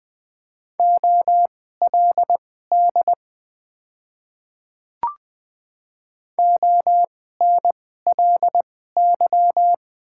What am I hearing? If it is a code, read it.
Morse code, heard as a keyed tone, about 20 wpm, spaced as for 14 wpm, 700 Hz. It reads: OLD ONLY
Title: 2x Character Spacing - Top 200 Words - 20wpm